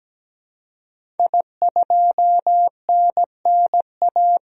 Morse code, heard as a keyed tone, about 17 wpm, 700 Hz. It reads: I2NNA
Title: DXCC-100 Call Signs - 5 Characters - 17wpm